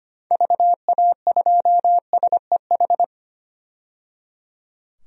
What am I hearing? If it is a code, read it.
Morse code, heard as a keyed tone, about 25 wpm, 700 Hz. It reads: VA2SEH
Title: DXCC-100 Call Signs - 6 Characters - 25wpm